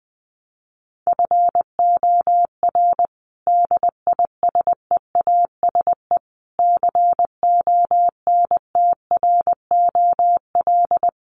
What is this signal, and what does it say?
Morse code, heard as a keyed tone, about 20 wpm, 700 Hz. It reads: FOR DISEASE CONTROL
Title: Sets of 5 Words Encore2 20wpm